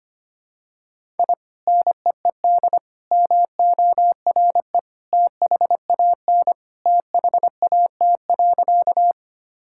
Morse code, heard as a keyed tone, about 25 wpm, 700 Hz. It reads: I NEED MORE THAN THAT.
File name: practice / 25wpm / Sentences with 2-Word Phrases - 25wpm